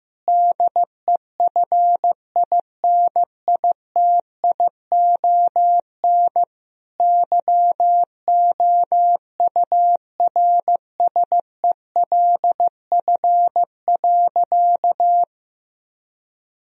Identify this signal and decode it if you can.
Morse code, heard as a keyed tone, about 16 wpm, 700 Hz. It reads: DEFINITION YOURSELF.
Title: Sentences from Top 1000 Words - Review Part 4 - 15wpm